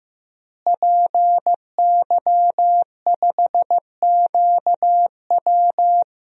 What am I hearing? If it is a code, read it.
Morse code, heard as a keyed tone, about 15 wpm, 700 Hz. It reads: PY5QW
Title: DXCC-100 Call Signs - 5 Characters - Repeated 3x - 15wpm